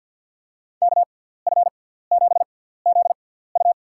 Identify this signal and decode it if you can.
Morse code, heard as a keyed tone, about 48 wpm, spaced as for 17 wpm, 700 Hz. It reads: KF7ZU
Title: USA Call Signs - 5 Characters - 6x Character Spacing - 50wpm